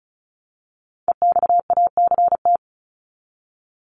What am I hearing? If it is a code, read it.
Morse code, heard as a keyed tone, about 34 wpm, 700 Hz. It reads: EXACT
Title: Top 1251-1500 Words 35wpm